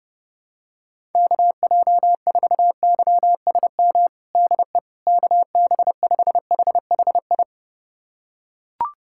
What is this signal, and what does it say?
Morse code, heard as a keyed tone, about 30 wpm, 700 Hz. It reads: KJ4YSM DE KB5HHI
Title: USA Callsign DE Callsign - 6 Characters - 30wpm